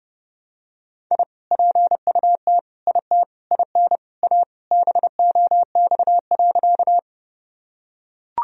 Morse code, heard as a keyed tone, about 32 wpm, 700 Hz. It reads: I PUT IT IN A BOX.
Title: Sentences from Top 1000 Words - Review Part 2 - 30wpm